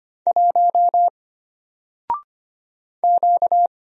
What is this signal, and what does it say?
Morse code, heard as a keyed tone, about 25 wpm, 700 Hz. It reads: 1 Q
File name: Single-Letter-Number-25wpm